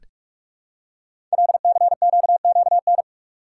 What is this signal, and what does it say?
Morse code, heard as a keyed tone, about 44 wpm, 700 Hz. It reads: LCQQN
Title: Code Groups - 5 Letters - 45wpm